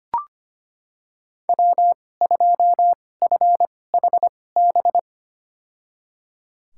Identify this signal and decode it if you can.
Morse code, heard as a keyed tone, about 25 wpm, spaced as for 18 wpm, 700 Hz. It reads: W2FHB
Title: USA Call Signs - 2x Character Spacing - 25wpm